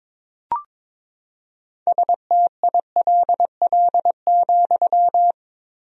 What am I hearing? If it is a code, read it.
Morse code, heard as a keyed tone, about 21 wpm, 700 Hz. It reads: STILL,
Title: 4x Word-Spacing - Sentences from Top 300 Words - 22wpm